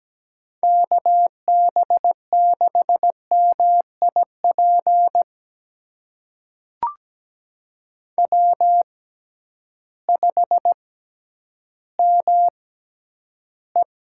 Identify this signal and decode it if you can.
Morse code, heard as a keyed tone, about 17 wpm, spaced as for 10 wpm, 700 Hz. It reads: KB6MIP W5ME
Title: USA Call Signs - 6x Character Spacing - 17wpm